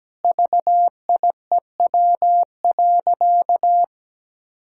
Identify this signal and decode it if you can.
Morse code, heard as a keyed tone, about 17 wpm, 700 Hz. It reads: VIEW.
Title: Sentences from Top 1000 Words - Review Part 6 - 17wpm